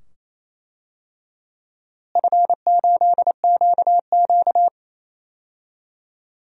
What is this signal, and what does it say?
Morse code, heard as a keyed tone, about 28 wpm, 700 Hz. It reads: F8QQ
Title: DXCC-100 Call Signs - 4 Characters - Repeated 2x - 28wpm